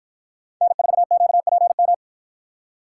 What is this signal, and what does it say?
Morse code, heard as a keyed tone, about 53 wpm, 700 Hz. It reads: N4QPM